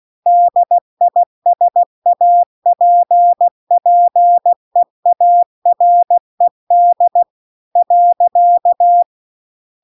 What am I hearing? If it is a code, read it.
Morse code, heard as a keyed tone, about 16 wpm, 700 Hz. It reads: DISAPPEARED .
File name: CW-The-Big-Twoheaded-River-2-16WPM